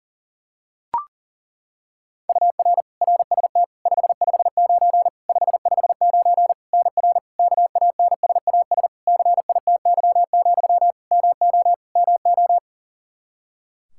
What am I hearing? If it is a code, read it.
Morse code, heard as a keyed tone, about 38 wpm, 700 Hz. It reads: UR RST 559 559 NR KANSAS CITY, MO MO
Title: K1USN SST QSO Practice 40wpm